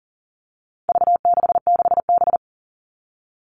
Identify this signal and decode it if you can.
Morse code, heard as a keyed tone, about 40 wpm, 700 Hz. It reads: V66B
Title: All Cut Numbers - 4-Digit 40wpm